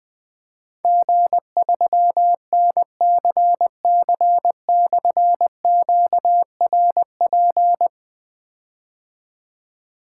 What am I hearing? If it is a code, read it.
Morse code, heard as a keyed tone, about 20 wpm, 700 Hz. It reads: G3NCC/QRP
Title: Difficult Call Signs 20wpm